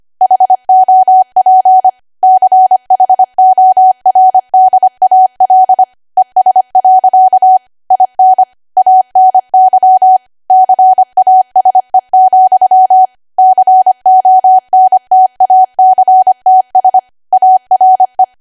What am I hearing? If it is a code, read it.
Morse code, heard as a keyed tone, about 25 wpm, 750 Hz. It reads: HOP CHORDAL ES. IN ANY CASE, CONTACTS ARE